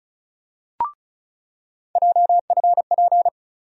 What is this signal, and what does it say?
Morse code, heard as a keyed tone, about 36 wpm, 700 Hz. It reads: JFP
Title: USA Call Sign Postfixes - 3 Letters 35wpm